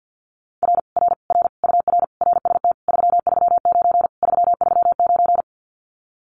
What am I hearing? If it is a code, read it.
Morse code, heard as a keyed tone, about 50 wpm, 700 Hz. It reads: R R R UR RST 339 339
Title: Common QSO Phrases Encore 50wpm